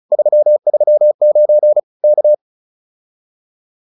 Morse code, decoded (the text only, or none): 339 K